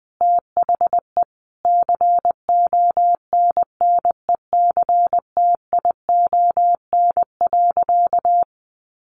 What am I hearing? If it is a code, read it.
Morse code, heard as a keyed tone, about 20 wpm, 700 Hz. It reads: THE CONNECTION.
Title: Sentences from Top 1300 Words 20wpm